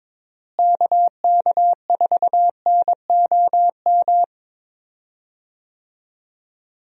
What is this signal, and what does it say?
Morse code, heard as a keyed tone, about 22 wpm, 700 Hz. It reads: KK4NOM
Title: USA Call Signs - 6 Characters - Repeated 3x - 22wpm